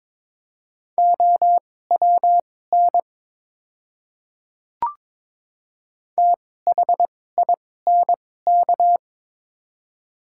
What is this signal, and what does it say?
Morse code, heard as a keyed tone, about 22 wpm, spaced as for 16 wpm, 700 Hz. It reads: OWN THINK
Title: 2x Character Spacing - Top 300 Words - 22wpm